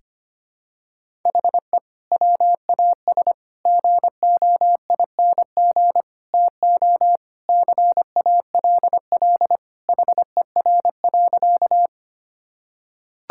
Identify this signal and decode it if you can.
Morse code, heard as a keyed tone, about 25 wpm, 700 Hz. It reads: HE WAS GOING TO CALL HER.
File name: practice / 25wpm / Sentences with 4-Word Phrases - 25wpm